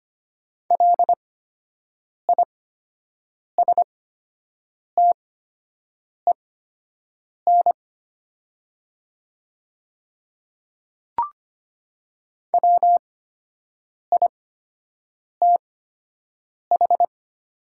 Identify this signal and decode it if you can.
Morse code, heard as a keyed tone, about 25 wpm, spaced as for 7 wpm, 700 Hz. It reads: LISTEN WITH
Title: 8x Character Spacing - Top 500 Words - 25wpm